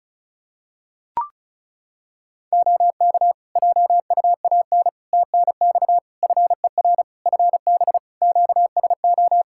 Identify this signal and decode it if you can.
Morse code, heard as a keyed tone, about 36 wpm, 700 Hz. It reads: OK JUAN TNX FER FB QSO
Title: Practice QSO - Exchange 4 - Ending - 35wpm